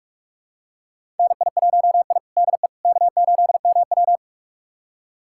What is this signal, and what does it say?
Morse code, heard as a keyed tone, about 46 wpm, 700 Hz. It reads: NI1I DE K8MW
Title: USA Callsign DE Callsign - 4 Characters - 45wpm